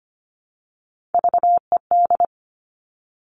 Morse code, heard as a keyed tone, about 25 wpm, 700 Hz. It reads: VED
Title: Top Three Letter Combinations in Words 25wpm